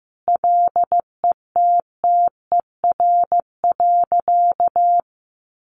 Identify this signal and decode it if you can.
Morse code, heard as a keyed tone, about 15 wpm, 700 Hz. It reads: LETTER.
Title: Sentences from Top 1400 Words 15wpm